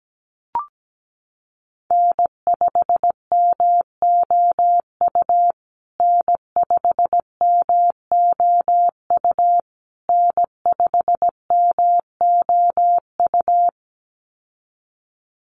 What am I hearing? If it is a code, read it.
Morse code, heard as a keyed tone, about 17 wpm, 700 Hz. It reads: N5MOU N5MOU N5MOU
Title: USA Call Signs Repeated 3x 17wpm